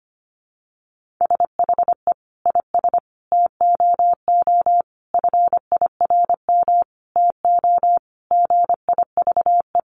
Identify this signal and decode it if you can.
Morse code, heard as a keyed tone, about 25 wpm, 700 Hz. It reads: SHE IS TOO FIRM TO GIVE
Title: Sentences from Top 1400 Words 25wpm